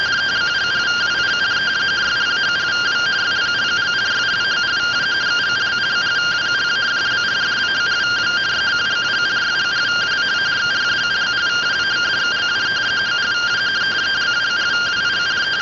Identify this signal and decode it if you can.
Two-tone teleprinter signal, about 170 Hz shift.